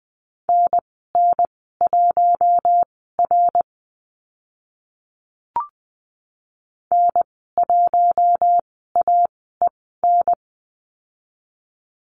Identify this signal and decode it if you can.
Morse code, heard as a keyed tone, about 20 wpm, spaced as for 14 wpm, 700 Hz. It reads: NN1R N1AEN
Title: 055 Lesson 10g - Letter r - Call Signs 20wpm